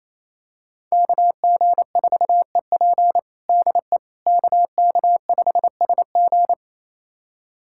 Morse code, heard as a keyed tone, about 28 wpm, 700 Hz. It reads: KG4EP DE KK5SG
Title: USA Callsign DE Callsign - 5 Characters - 28wpm